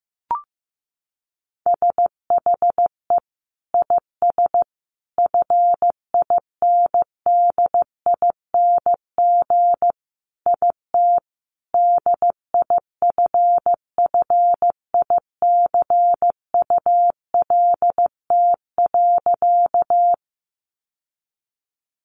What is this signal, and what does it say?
Morse code, heard as a keyed tone, about 15 wpm, 700 Hz. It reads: SHE IS FINDING IT DIFFICULT.
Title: Sentences from Top 1100 Words 15wpm